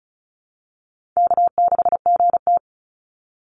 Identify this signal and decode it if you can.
Morse code, heard as a keyed tone, about 34 wpm, 700 Hz. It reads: K6GT
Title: Call Signs Repeated 2x - 35wpm